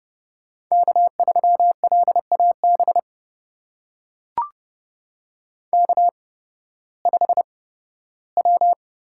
Morse code, heard as a keyed tone, about 30 wpm, spaced as for 16 wpm, 700 Hz. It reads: K3LAB K5W
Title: USA Call Signs - 5 Characters - 8x Character Spacing - 30wpm